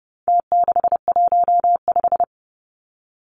Morse code, heard as a keyed tone, about 30 wpm, 700 Hz. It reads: T615
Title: Numbers - Contest Style - 4-Digit 30wpm